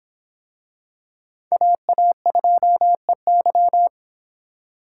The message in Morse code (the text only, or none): AA2EY